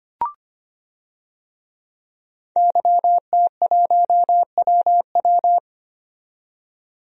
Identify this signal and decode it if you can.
Morse code, heard as a keyed tone, about 25 wpm, 700 Hz. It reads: YT1WW
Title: DXCC-100 Call Signs - 5 Characters - 25wpm